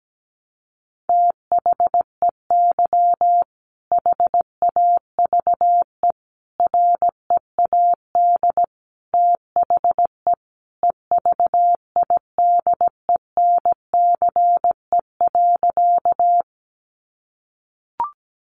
Morse code, heard as a keyed tone, about 17 wpm, 700 Hz. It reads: THEY HAVE READ THE EVIDENCE.
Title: Sentences from Top 1300 Words 17wpm